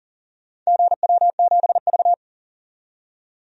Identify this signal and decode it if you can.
Morse code, heard as a keyed tone, about 40 wpm, 700 Hz. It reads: GW7V
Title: DXCC-100 Call Signs - 4 Characters - 40wpm